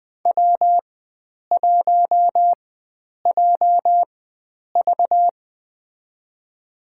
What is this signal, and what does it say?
Morse code, heard as a keyed tone, about 20 wpm, spaced as for 9 wpm, 700 Hz. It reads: W1JV
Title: USA Call Signs - 4 Characters - 4x Character Spacing - 20wpm